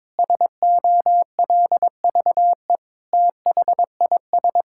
Morse code, heard as a keyed tone, about 22 wpm, 700 Hz. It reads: SOLVE THIS
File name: Sentences with 7-Word Phrases - 22wpm